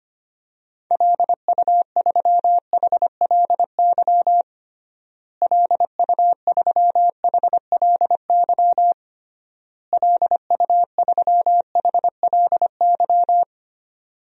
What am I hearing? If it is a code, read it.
Morse code, heard as a keyed tone, about 25 wpm, 700 Hz. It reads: LU3HLY LU3HLY LU3HLY
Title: DXCC-100 Call Signs - 6 Characters - Repeated 3x - 25wpm